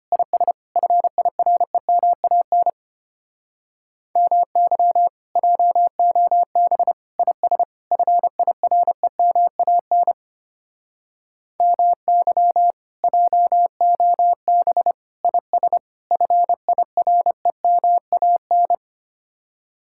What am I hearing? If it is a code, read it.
Morse code, heard as a keyed tone, about 33 wpm, 700 Hz. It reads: IS FIREMAN MY JOB IS FIREMAN MY JOB IS FIREMAN